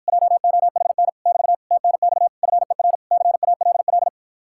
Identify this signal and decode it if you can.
Morse code, heard as a keyed tone, about 53 wpm, 700 Hz. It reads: JOHN <BT> TNX FER CALL